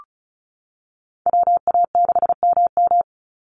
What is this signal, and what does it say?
Morse code, heard as a keyed tone, about 34 wpm, 700 Hz. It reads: WA6MM
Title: NA SOTA Chaser Callsigns 35wpm